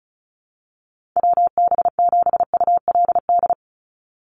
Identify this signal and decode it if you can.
Morse code, heard as a keyed tone, about 34 wpm, 700 Hz. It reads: WB7ULD
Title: NA SOTA Chaser Callsigns 35wpm